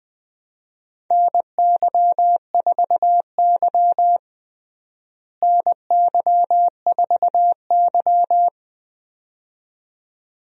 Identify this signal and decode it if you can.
Morse code, heard as a keyed tone, about 20 wpm, 700 Hz. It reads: NY4Y NY4Y
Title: USA Call Signs - 4 Characters - Repeated 2x - 20wpm